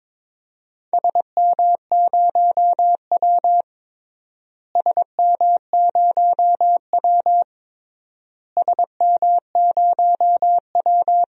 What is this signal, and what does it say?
Morse code, heard as a keyed tone, about 22 wpm, 700 Hz. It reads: SM0W SM0W SM0W
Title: DXCC-100 Call Signs - 4 Characters - Repeated 3x - 22wpm